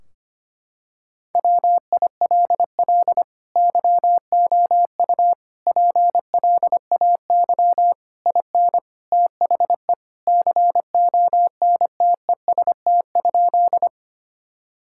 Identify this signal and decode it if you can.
Morse code, heard as a keyed tone, about 25 wpm, 700 Hz. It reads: WILL YOU PLAY IN THE CONTEST?
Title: Sentences from Top 1600 Words 25wpm